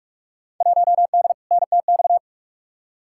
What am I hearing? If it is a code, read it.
Morse code, heard as a keyed tone, about 46 wpm, 700 Hz. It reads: 1D NTX
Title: Field Day Exchanges - 45wpm